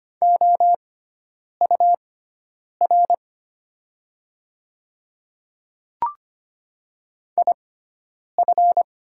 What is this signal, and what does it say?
Morse code, heard as a keyed tone, about 25 wpm, spaced as for 9 wpm, 700 Hz. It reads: OUR IF
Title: 6x Character Spacing - Top 200 Words - 25wpm